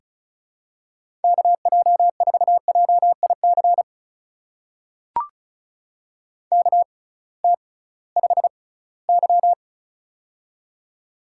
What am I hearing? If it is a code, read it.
Morse code, heard as a keyed tone, about 36 wpm, spaced as for 21 wpm, 700 Hz. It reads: KJ4JIC KT5Y
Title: USA Call Signs - 6x Character Spacing - 35wpm